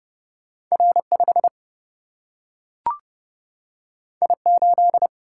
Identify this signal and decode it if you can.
Morse code, heard as a keyed tone, about 30 wpm, 700 Hz. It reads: R 5 I 8
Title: Single-Letter Single-Number 30wpm